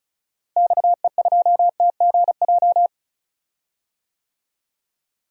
Morse code, heard as a keyed tone, about 36 wpm, 700 Hz. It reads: XE2TGJ
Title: DXCC-100 Call Signs - 6 Characters - Repeated 2x - 35wpm